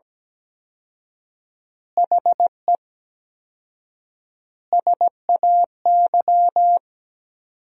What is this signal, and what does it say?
Morse code, heard as a keyed tone, about 17 wpm, 700 Hz. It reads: HE SAY
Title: 4x Word-Spacing - Sentences from Top 300 Words - 17wpm